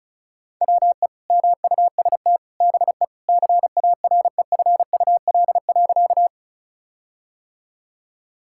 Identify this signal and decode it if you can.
Morse code, heard as a keyed tone, about 36 wpm, 700 Hz. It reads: WE MUST BE CAREFUL.
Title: Sentences from Top 1600 Words 35wpm